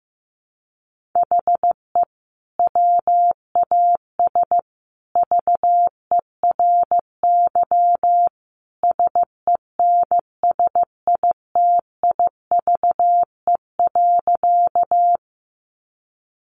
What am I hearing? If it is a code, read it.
Morse code, heard as a keyed tone, about 15 wpm, 700 Hz. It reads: HE WAS VERY SENSITIVE.
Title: Sentences from Top 1400 Words 15wpm